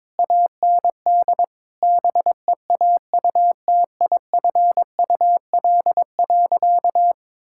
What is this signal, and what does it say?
Morse code, heard as a keyed tone, about 22 wpm, 700 Hz. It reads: AND BEAUTIFUL.
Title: Sentences from Top 1000 Words - Review Part 5 - 22wpm